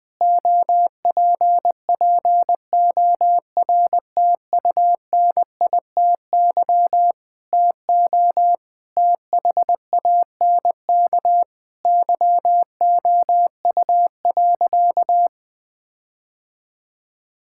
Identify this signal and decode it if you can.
Morse code, heard as a keyed tone, about 19 wpm, 700 Hz. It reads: OPPORTUNITY TO THANK YOU.
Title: Sentences with 7-Word Phrases - 20wpm